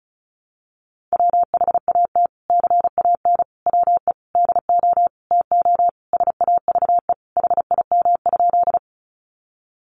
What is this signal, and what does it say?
Morse code, heard as a keyed tone, about 34 wpm, 700 Hz. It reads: WHAT CAN WE DO TO SAVE HIM?
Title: Sentences from Top 1100 Words 35wpm